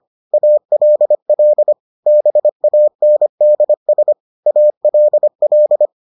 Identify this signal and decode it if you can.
Morse code, heard as a keyed tone, about 25 wpm, 600 Hz. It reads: ALL BANDS ALL